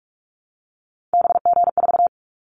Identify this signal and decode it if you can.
Morse code, heard as a keyed tone, about 46 wpm, 700 Hz. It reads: BG4
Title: Callsign Prefixes for Top 100 DXCC - 45wpm